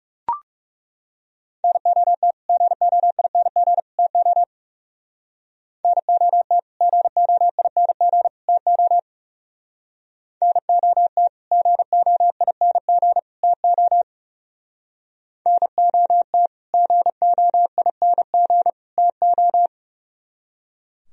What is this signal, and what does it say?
Morse code, heard as a keyed tone, about 37 wpm, 700 Hz. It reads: NOT GOING TO NOT GOING TO NOT GOING TO NOT GOING TO
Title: Speed-Racing - Sets of 3 Words 30wpm